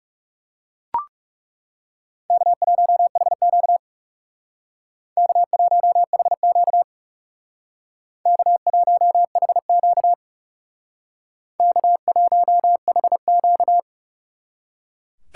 Morse code, heard as a keyed tone, about 37 wpm, 700 Hz. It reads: K1HQ K1HQ K1HQ K1HQ
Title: Speed Racing - USA 1x2 Call Signs 30wpm